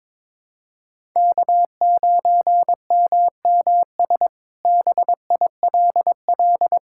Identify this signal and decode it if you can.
Morse code, heard as a keyed tone, about 22 wpm, 700 Hz. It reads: K9MMS BILL
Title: K1USN SST QSO Practice - 22wpm